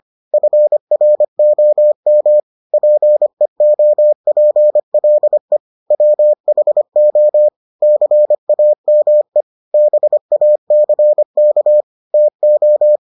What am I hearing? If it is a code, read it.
Morse code, heard as a keyed tone, about 25 wpm, 600 Hz. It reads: FROM PEOPLE WHO CAME BACK TO